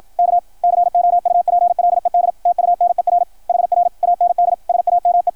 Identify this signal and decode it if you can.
Morse code, heard as a keyed tone, about 53 wpm, 690 Hz, with radio noise. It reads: K COUPLED TUNER HR AND SAME
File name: archived / CRSnet